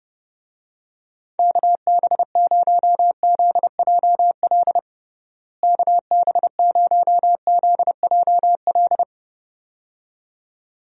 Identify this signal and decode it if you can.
Morse code, heard as a keyed tone, about 30 wpm, 700 Hz. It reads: KB0ZJL KB0ZJL
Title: USA Call Signs - 6 Characters - Repeated 2x - 30wpm